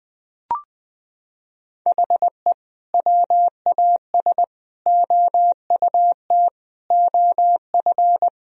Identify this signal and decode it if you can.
Morse code, heard as a keyed tone, about 20 wpm, 700 Hz. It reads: HE WAS OUT OF
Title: Sentences from Top 1000 Words - Review Part 4 - 20wpm